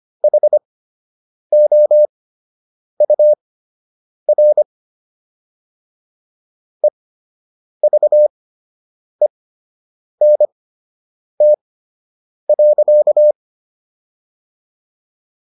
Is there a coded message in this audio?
Yes: Morse code from 0.2 to 13.3 s, heard as a keyed tone, about 25 wpm, spaced as for 8 wpm, 600 Hz. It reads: HOUR EVENT.